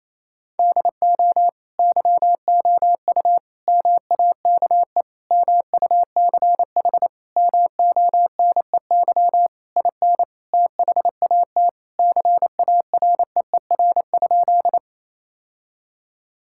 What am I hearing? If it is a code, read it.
Morse code, heard as a keyed tone, about 28 wpm, 700 Hz. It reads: DO YOU MAKE MUCH MONEY IN THAT CAREER?
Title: Sentences from Top 1000 Words - Review Part 6 - 28wpm